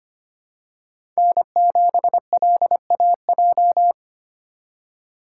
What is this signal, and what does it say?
Morse code, heard as a keyed tone, about 25 wpm, 700 Hz. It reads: N7LAJ
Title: USA Call Signs - 5 Characters - 4x Character Spacing - 25wpm